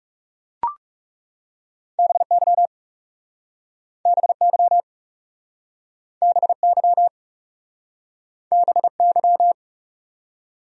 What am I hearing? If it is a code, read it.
Morse code, heard as a keyed tone, about 41 wpm, 700 Hz. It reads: BY BY BY BY